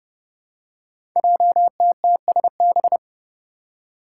Morse code, heard as a keyed tone, about 28 wpm, 700 Hz. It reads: JTTSB
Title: Five Challenging Letter-Number Combinations 30wpm